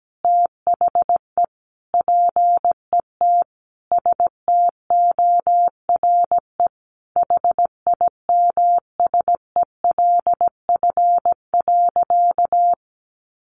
Morse code, heard as a keyed tone, about 17 wpm, 700 Hz. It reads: THE PET STORE HIMSELF.
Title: Additional Easy Phrases 17wpm